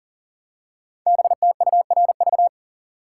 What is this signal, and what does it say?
Morse code, heard as a keyed tone, about 38 wpm, 690 Hz. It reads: BTURV
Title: Code Groups - 5 Letters - 40wpm